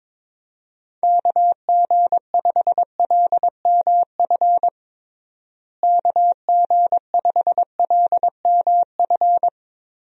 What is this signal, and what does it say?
Morse code, heard as a keyed tone, about 22 wpm, 700 Hz. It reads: KG5LMF KG5LMF
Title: USA Call Signs - 6 Characters - Repeated 2x - 22wpm